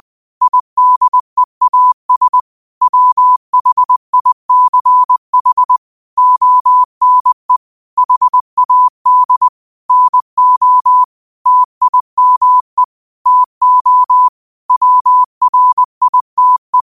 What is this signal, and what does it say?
Morse code, heard as a keyed tone, about 20 wpm, 1000 Hz. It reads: IDEAS WHICH ONE HAD NO TIME TO WRITE